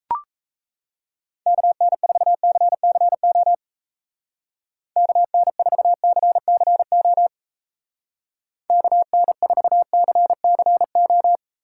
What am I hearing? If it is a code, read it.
Morse code, heard as a keyed tone, about 35 wpm, 700 Hz. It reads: KN4CCO KN4CCO KN4CCO
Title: Speed Racing - USA Call Signs 28wpm